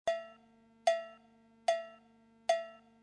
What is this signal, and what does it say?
There is no code.